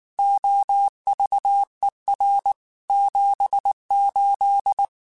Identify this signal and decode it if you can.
Morse code, heard as a keyed tone, about 19 wpm, 800 Hz. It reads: OVER 78
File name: QSOs / QSOs-mp3s / kc7haa@18